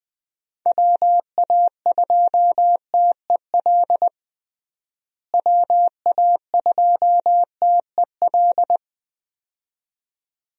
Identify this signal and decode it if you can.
Morse code, heard as a keyed tone, about 21 wpm, 700 Hz. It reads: WA2TEL WA2TEL
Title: USA Call Signs - 6 Characters - Repeated 2x - 20wpm